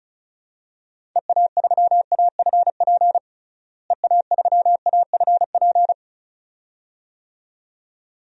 Morse code, heard as a keyed tone, about 34 wpm, 700 Hz. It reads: EA3AFP EA3AFP
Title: DXCC-100 Call Signs - 6 Characters - Repeated 2x - 35wpm